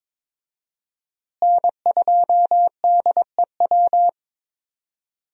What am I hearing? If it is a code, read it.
Morse code, heard as a keyed tone, about 21 wpm, 700 Hz. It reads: N2DEW
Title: USA Call Signs - 5 Characters - Repeated 3x - 22wpm